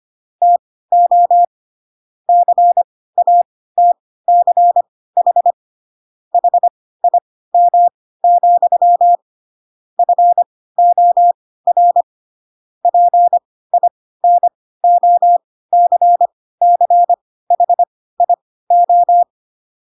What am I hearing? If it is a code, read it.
Morse code, heard as a keyed tone, about 25 wpm, spaced as for 16 wpm, 700 Hz. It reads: TO CATCH HIM, FOR PINOCCHIO
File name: Pinocchio / Pinocchio ch3 10 25 ik1hge